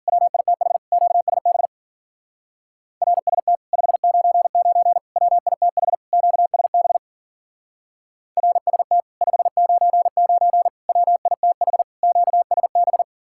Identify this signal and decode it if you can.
Morse code, heard as a keyed tone, about 49 wpm, 700 Hz. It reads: WITH QSB RST 599 WITH QSB RST 599 WITH QSB
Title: Speed-Racing Common QSO Phrases - 40wpm